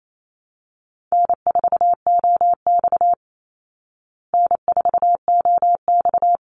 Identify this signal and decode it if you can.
Morse code, heard as a keyed tone, about 28 wpm, 700 Hz. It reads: N4OX N4OX
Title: Easy Call Signs Repeated 3x - 28wpm